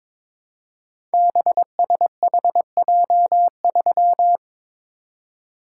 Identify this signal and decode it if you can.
Morse code, heard as a keyed tone, about 23 wpm, 700 Hz. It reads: BSHJ3
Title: Code Groups - 5 Letter:Numbers - 22wpm